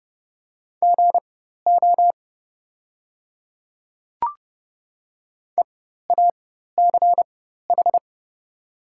Morse code, heard as a keyed tone, about 30 wpm, spaced as for 14 wpm, 700 Hz. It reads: GO EACH
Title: 4x Character Spacing - Top 100 Words - 30wpm